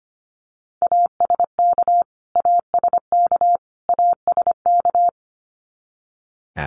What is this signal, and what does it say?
Morse code, heard as a keyed tone, about 25 wpm, 700 Hz. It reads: ASK ASK ASK
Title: Top 500 Words Repeated 25wpm